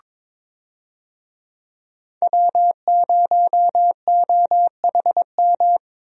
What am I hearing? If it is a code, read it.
Morse code, heard as a keyed tone, about 22 wpm, 700 Hz. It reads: W0OHM
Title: USA Call Signs - 5 Characters - Repeated 2x - 22wpm